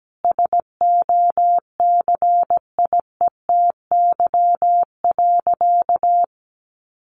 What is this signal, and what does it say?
Morse code, heard as a keyed tone, about 17 wpm, 700 Hz. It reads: SOCIETY.
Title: Sentences from Top 700 Words 17wpm